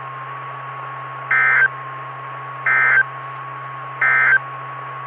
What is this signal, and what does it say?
Emergency Alert System (SAME) tones, heard as harsh buzzing bursts. It is heard from 1.2 to 4.5 s.
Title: Saline, AK SKYWARN Repeater System - W5RHS